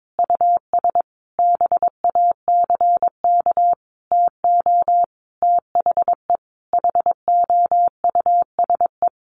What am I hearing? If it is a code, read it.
Morse code, heard as a keyed tone, about 22 wpm, 700 Hz. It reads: US BACK TO THE HOUSE
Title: Sentences from Top 1500 Words 22wpm